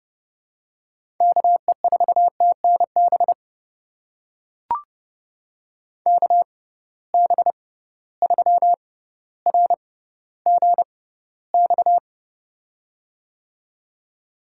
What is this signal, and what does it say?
Morse code, heard as a keyed tone, about 30 wpm, spaced as for 15 wpm, 700 Hz. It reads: KE4TNB KB3RGX
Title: USA Call Signs - 6 Characters - 6x Character Spacing - 30wpm